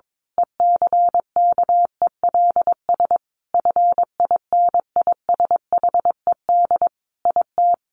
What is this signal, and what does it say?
Morse code, heard as a keyed tone, about 22 wpm, 700 Hz. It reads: ECKELS FINISHED IT
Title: A Sound Of Thunder 22wpm